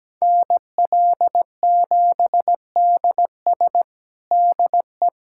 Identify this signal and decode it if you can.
Morse code, heard as a keyed tone, about 17 wpm, 700 Hz. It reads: NL7DS DE
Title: USA Callsign DE Callsign - 5 Characters - 17wpm